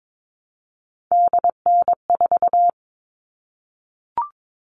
Morse code, heard as a keyed tone, about 22 wpm, 700 Hz. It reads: DN4
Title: All Cut Numbers - 3-Digit 22wpm